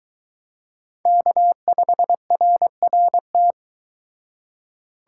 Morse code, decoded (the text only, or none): K5RRT